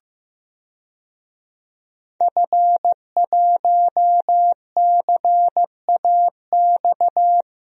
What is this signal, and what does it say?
Morse code, heard as a keyed tone, about 15 wpm, 700 Hz. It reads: F1CAX
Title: DXCC-100 Call Signs - 5 Characters - Repeated 2x - 15wpm